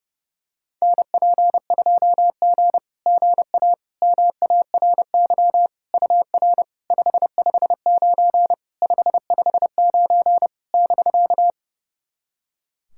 Morse code, heard as a keyed tone, about 30 wpm, 700 Hz. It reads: NP2G GA MARY UR 559 559 <BK>
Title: SOTA QSO Practice 30wpm